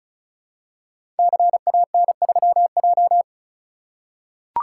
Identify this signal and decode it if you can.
Morse code, heard as a keyed tone, about 36 wpm, 700 Hz. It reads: CAN3J
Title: Code Groups - 5 Letter:Numbers - 35wpm